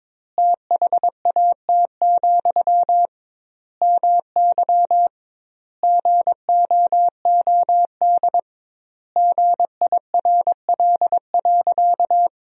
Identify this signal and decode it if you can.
Morse code, heard as a keyed tone, about 22 wpm, 700 Hz. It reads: THAT, MY GOOD GIRL.
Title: 2x Word-Spacing - Sentences from Top 400 Words - 22wpm